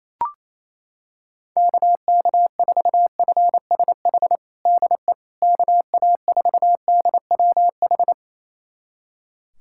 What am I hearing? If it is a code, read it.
Morse code, heard as a keyed tone, about 28 wpm, 700 Hz. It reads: KK4FSH DE KA4DWH
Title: USA Callsign DE Callsign - 6 Characters - 28wpm